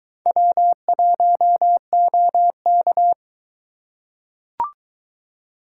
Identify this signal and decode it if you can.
Morse code, heard as a keyed tone, about 21 wpm, 700 Hz. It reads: W1OK